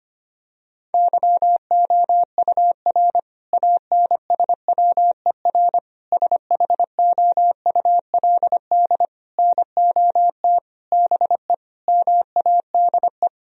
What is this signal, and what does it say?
Morse code, heard as a keyed tone, about 25 wpm, 700 Hz. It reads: YOUR ANSWER SHOULD NOT BE MADE
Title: Sentences from Top 1000 Words - Review Part 3 - 25wpm